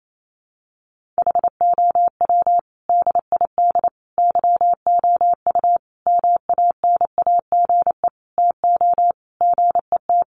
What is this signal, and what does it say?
Morse code, heard as a keyed tone, about 28 wpm, 700 Hz. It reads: HOW DID YOU MANAGE TO GET
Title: Sentences from Top 1400 Words 28wpm